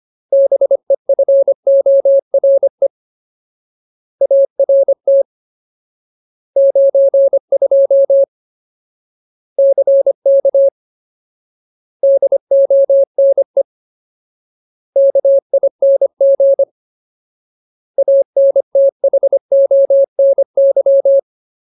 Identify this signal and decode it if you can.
Morse code, heard as a keyed tone, about 25 wpm, 550 Hz. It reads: BEFORE ART 92 CK DONE KING ANTHONY
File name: Ultimate Word List / 25WPM Ultimate